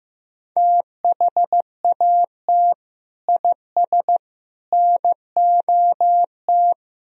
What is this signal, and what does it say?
Morse code, heard as a keyed tone, about 16 wpm, 700 Hz. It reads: THAT IS NOT
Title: Sentences from Top 1000 Words - Review Part 4 - 15wpm